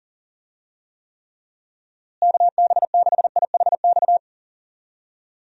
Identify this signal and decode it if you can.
Morse code, heard as a keyed tone, about 38 wpm, 700 Hz. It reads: KB6IHX
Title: USA Call Signs - 6 Characters - Repeated 3x - 40wpm